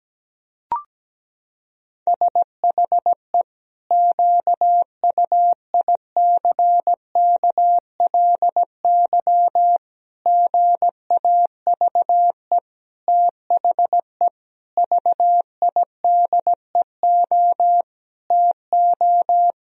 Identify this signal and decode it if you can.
Morse code, heard as a keyed tone, about 17 wpm, 700 Hz. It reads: SHE QUICKLY GAVE THE VIDEO TO
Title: Sentences from Top 1000 Words - Review Part 4 - 17wpm